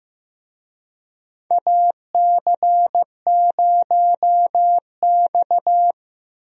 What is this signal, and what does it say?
Morse code, heard as a keyed tone, about 15 wpm, 700 Hz. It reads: AC0X
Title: USA Call Signs - 4 Characters - Repeated 3x - 15wpm